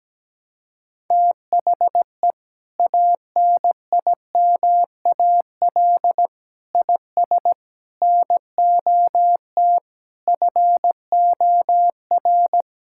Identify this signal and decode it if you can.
Morse code, heard as a keyed tone, about 17 wpm, 700 Hz. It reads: THE ANIMAL IS NOT FOR
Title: Sentences from Top 1000 Words - Review Part 5 - 17wpm